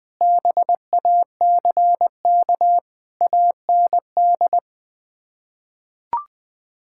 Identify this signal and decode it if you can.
Morse code, heard as a keyed tone, about 21 wpm, 700 Hz. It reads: BACK AND
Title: Sets of 6 Words - 20wpm